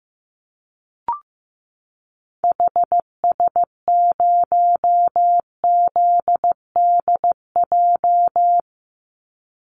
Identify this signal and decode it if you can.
Morse code, heard as a keyed tone, about 14 wpm, 700 Hz. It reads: HS0ZDJ
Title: Call Signs Repeated 3x - 15wpm